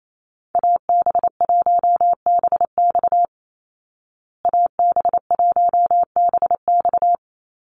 Morse code, heard as a keyed tone, about 28 wpm, 700 Hz. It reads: AB1BX AB1BX
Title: Call Signs Repeated 3x - 28wpm